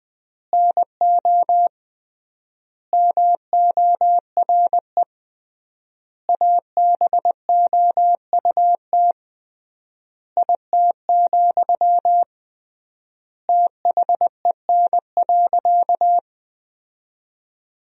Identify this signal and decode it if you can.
Morse code, heard as a keyed tone, about 20 wpm, 700 Hz. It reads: NO MORE ABOUT IT, THEN.
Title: 3x Word-Spacing - Sentences from Top 200 Words - 20wpm